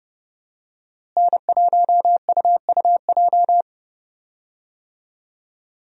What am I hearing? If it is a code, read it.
Morse code, heard as a keyed tone, about 30 wpm, 700 Hz. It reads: N1UUJ
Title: USA Call Signs - 5 Characters - Repeated 3x - 30wpm